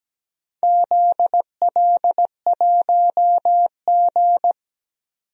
Morse code, heard as a keyed tone, about 17 wpm, 700 Hz. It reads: ZL1G
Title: DXCC-100 Call Signs - 4 Characters - 17wpm